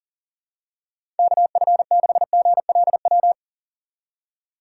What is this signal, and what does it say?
Morse code, heard as a keyed tone, about 40 wpm, 700 Hz. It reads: KF6GLW
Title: USA Call Signs - 4x Character Spacing - 40wpm